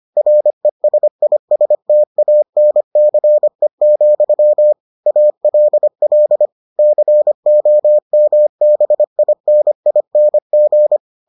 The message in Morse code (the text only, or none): RESISTANCE, ALL COMBINING